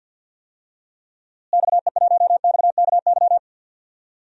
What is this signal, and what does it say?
Morse code, heard as a keyed tone, about 51 wpm, 700 Hz. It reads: XE1XKY
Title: DXCC-100 Call Signs - 6 Characters - Repeated 3x - 50wpm